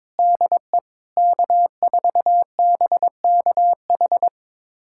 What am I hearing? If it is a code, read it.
Morse code, heard as a keyed tone, about 23 wpm, 700 Hz. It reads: DE K4BKH
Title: USA Callsign DE Callsign 22wpm